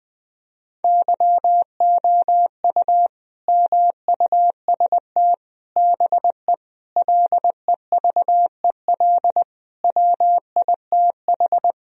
Morse code, heard as a keyed tone, about 20 wpm, 700 Hz. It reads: YOU MUST BE LEVEL WITH
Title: Sentences from Top 1000 Words - Review Part 4 - 20wpm